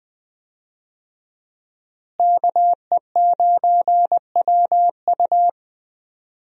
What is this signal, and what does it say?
Morse code, heard as a keyed tone, about 20 wpm, 700 Hz. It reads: KE9WU
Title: USA Call Signs - 5 Characters - Repeated 2x - 20wpm